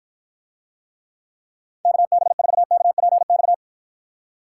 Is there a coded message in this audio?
Yes: Morse code from 1.9 to 3.5 s, heard as a keyed tone, about 54 wpm, 700 Hz. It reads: KB4KPX